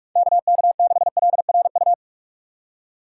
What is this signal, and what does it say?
Morse code, heard as a keyed tone, about 46 wpm, 700 Hz. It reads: KK6LRU
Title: USA Call Signs - 6 Characters - 3x Character Spacing - 45wpm